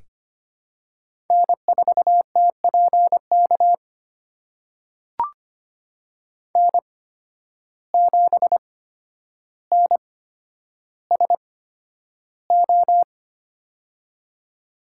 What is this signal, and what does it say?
Morse code, heard as a keyed tone, about 25 wpm, spaced as for 11 wpm, 700 Hz. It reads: N4TPK N7NSO
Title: USA Call Signs - 5 Characters - 8x Character Spacing - 25wpm